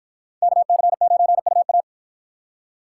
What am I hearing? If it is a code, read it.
Morse code, heard as a keyed tone, about 49 wpm, 700 Hz. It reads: KC9UA